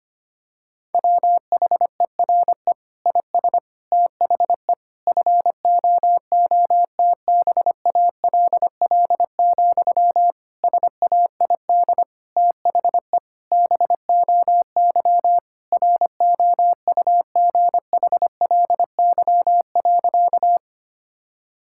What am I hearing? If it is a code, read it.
Morse code, heard as a keyed tone, about 25 wpm, 700 Hz. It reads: WHERE IS THE FOOTBALL, SAID THE BOY ROUGHLY.
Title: Sentences from Top 1600 Words 25wpm